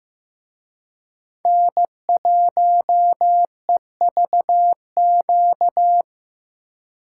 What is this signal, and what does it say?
Morse code, heard as a keyed tone, about 18 wpm, 700 Hz. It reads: N1EVQ
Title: Speed Racing - USA 1x3 Call Signs 15wpm